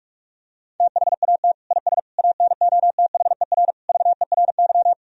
Morse code, heard as a keyed tone, about 41 wpm, 700 Hz. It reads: THAT IS ANOTHER VERY
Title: Sentences from Top 1000 Words - Review Part 2 - 45wpm